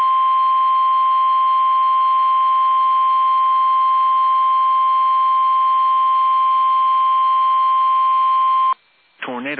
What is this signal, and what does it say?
Emergency Alert System (SAME) tones (the 1050 Hz attention tone), with speech. It is heard from 0.4 to 8.7 s.
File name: eas / KUSO / KUSO